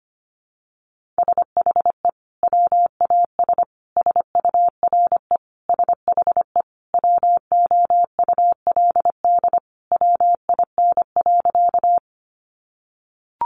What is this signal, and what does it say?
Morse code, heard as a keyed tone, about 25 wpm, 700 Hz. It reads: SHE WAS SURE SHE WOULD WIN.
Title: Sentences from Top 1500 Words 25wpm